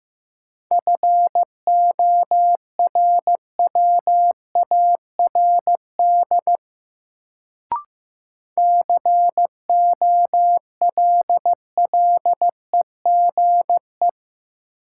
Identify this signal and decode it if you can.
Morse code, heard as a keyed tone, about 15 wpm, 700 Hz. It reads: FORWARD COLLEGE
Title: Sentences from Top 1300 Words - New Word Drill Alt - 15wpm